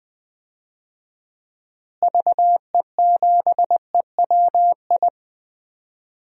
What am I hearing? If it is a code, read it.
Morse code, heard as a keyed tone, about 20 wpm, 700 Hz. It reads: VE7EWI
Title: DXCC-100 Call Signs - 6 Characters - Repeated 2x - 20wpm